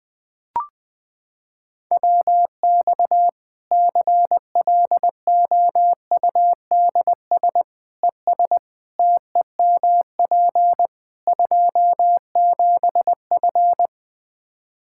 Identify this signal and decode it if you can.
Morse code, heard as a keyed tone, about 20 wpm, 700 Hz. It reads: WX CLOUDS ES TEMP 27F
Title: Practice QSO - Exchange 2 - Info - 20wpm